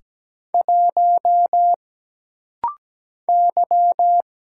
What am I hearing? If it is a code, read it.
Morse code, heard as a keyed tone, about 17 wpm, 700 Hz. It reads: 1 Y
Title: Single Letter-Number Alt - Rapid-Fire 17wpm